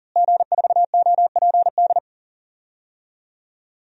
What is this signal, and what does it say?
Morse code, heard as a keyed tone, about 40 wpm, 700 Hz. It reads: G4OPD
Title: DXCC-100 Call Signs - 5 Characters - Repeated 3x - 40wpm